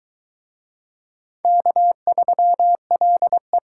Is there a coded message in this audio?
Yes: Morse code from 1.4 to 3.6 s, heard as a keyed tone, about 22 wpm, 700 Hz. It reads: K3LE